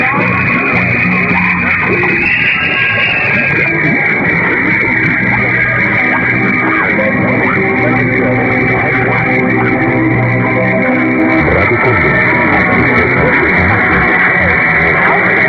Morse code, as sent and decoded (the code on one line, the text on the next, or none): none